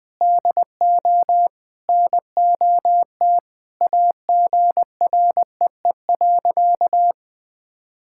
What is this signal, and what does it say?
Morse code, heard as a keyed tone, about 20 wpm, 700 Hz. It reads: DO NOT AGREE.
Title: Sentences with 4-Word Phrases - Spoken Beginning - 20wpm